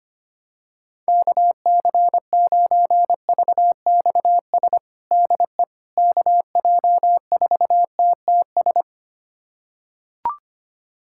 Morse code, heard as a keyed tone, about 25 wpm, 700 Hz. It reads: KC9VXS DE KJ4TTS
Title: USA Callsign DE Callsign - 6 Characters - 25wpm